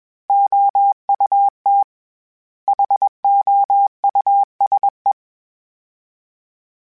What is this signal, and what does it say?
Morse code, heard as a keyed tone, about 21 wpm, 800 Hz. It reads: OUT HOUSE
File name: General / General-mp3s / Houses @20